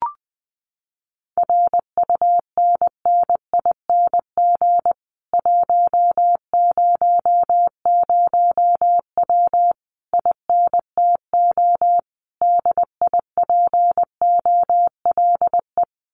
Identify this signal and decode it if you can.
Morse code, heard as a keyed tone, about 20 wpm, 700 Hz. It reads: RUNNING 100W INTO DIPOLE
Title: Common QSO Phrases Encore 20wpm